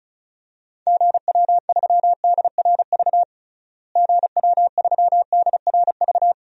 Morse code, heard as a keyed tone, about 34 wpm, 700 Hz. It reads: GW3DRV GW3DRV
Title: DXCC-100 Call Signs - 6 Characters - Repeated 3x - 35wpm